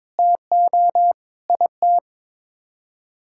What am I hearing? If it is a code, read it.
Morse code, heard as a keyed tone, about 22 wpm, 700 Hz. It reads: TO IT
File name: practice / 22wpm / Sets of 6 Words - 22wpm